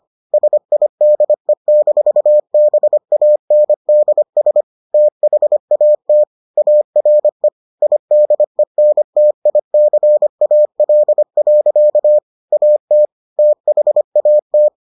Morse code, heard as a keyed tone, about 25 wpm, 600 Hz. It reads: SIDE-BANDS THAT ARE IDENTICAL. AT THAT